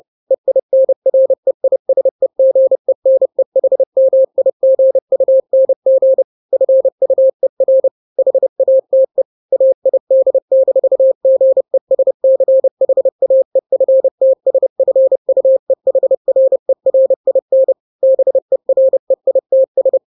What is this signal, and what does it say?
Morse code, heard as a keyed tone, about 30 wpm, 530 Hz. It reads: EINREISEGENEHMIGUNG FUER HATE AID-GESCHAEFTSFUEHRERIN BEREITS
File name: daily / 29wpm